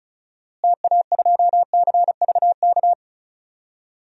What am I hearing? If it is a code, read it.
Morse code, heard as a keyed tone, about 34 wpm, 700 Hz. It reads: TA2CVK
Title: DXCC-100 Call Signs - 6 Characters - Repeated 3x - 35wpm